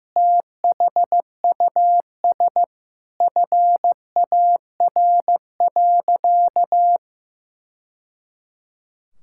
Morse code, heard as a keyed tone, about 15 wpm, 700 Hz. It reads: THUS FAR.
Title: Sentences from Top 1000 Words - Review Part 5 - 15wpm